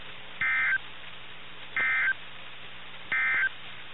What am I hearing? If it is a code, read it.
Emergency Alert System (SAME) tones, heard as harsh buzzing bursts. It is heard from 0.3 to 3.6 s.
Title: NOAA Weather Radio - KIH61